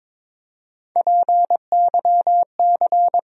Morse code, heard as a keyed tone, about 22 wpm, 700 Hz. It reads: PYC
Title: USA Call Sign Postfixes - 3 Letters 22wpm